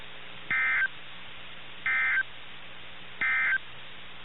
Emergency Alert System (SAME) tones, heard as harsh buzzing bursts. It is heard from 0.4 to 3.7 s.
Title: NOAA Weather Radio - KIH61